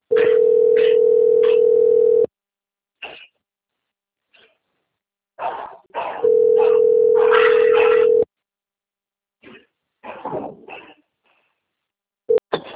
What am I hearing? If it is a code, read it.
Ringback tone (North America), 440 + 480 Hz.